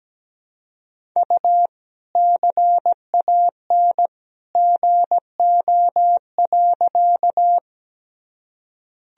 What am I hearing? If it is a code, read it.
Morse code, heard as a keyed tone, about 17 wpm, 700 Hz. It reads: U CAN GO.
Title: Sentences from Top 100 Words and Abbreviations 17wpm